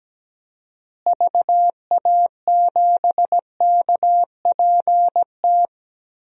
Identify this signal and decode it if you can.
Morse code, heard as a keyed tone, about 17 wpm, 700 Hz. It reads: VA7KPT
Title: DXCC-100 Call Signs - 6 Characters - Repeated 2x - 17wpm